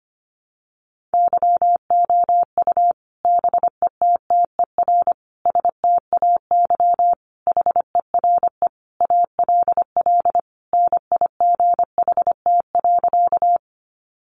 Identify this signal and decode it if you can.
Morse code, heard as a keyed tone, about 25 wpm, 700 Hz. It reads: YOU BETTER STAY HERE ALL NIGHT.
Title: Sentences from Top 500 Words 25wpm